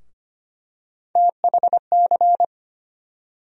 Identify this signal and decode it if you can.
Morse code, heard as a keyed tone, about 25 wpm, 700 Hz. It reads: THC
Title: USA Call Sign Postfixes - 3 Letters 25wpm